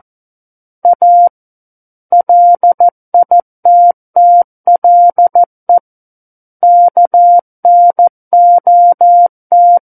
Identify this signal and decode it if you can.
Morse code, heard as a keyed tone, about 14 wpm, 700 Hz. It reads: A LITTLE KNOT
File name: Book One / 05-Chapter 5 - 14 WPM